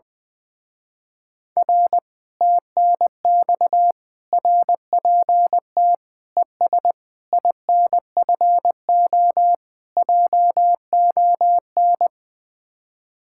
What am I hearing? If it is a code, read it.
Morse code, heard as a keyed tone, about 20 wpm, 700 Hz. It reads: R TNX RPT ES INFO JON
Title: Speed-Racing Common QSO Phrases - 20wpm